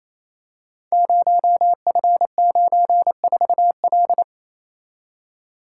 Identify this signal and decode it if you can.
Morse code, heard as a keyed tone, about 29 wpm, 700 Hz. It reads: 0F94L
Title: Code Groups - 5 Letter:Numbers - 28wpm